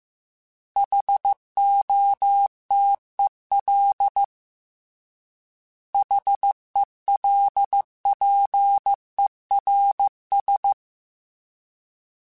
Morse code, heard as a keyed tone, about 15 wpm, 800 Hz. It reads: HOTEL HELPERS